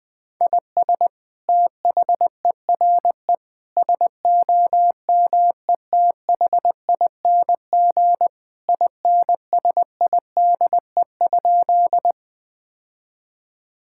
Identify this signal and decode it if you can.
Morse code, heard as a keyed tone, about 20 wpm, 700 Hz. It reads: IS THERE SOMETHING INSIDE?
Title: Sentences from Top 1000 Words - Review Part 6 - 20wpm